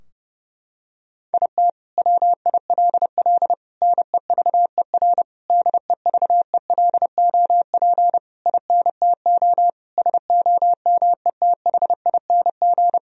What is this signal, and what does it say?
Morse code, heard as a keyed tone, about 30 wpm, 700 Hz. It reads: IT WILL NEVER DEVELOP INTO SOMETHING
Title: Sentences from Top 1000 Words - Review Part 2 - 30wpm